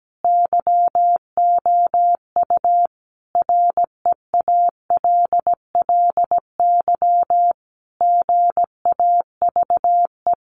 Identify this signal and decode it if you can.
Morse code, heard as a keyed tone, about 17 wpm, 700 Hz. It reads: YOU REALLY GAVE
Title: Sentences from Top 900 Words 17wpm